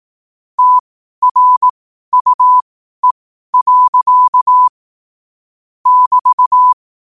Morse code, heard as a keyed tone, about 18 wpm, spaced as for 12 wpm, 1000 Hz. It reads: TRUE. <BT>
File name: qotdfiles / 12wpm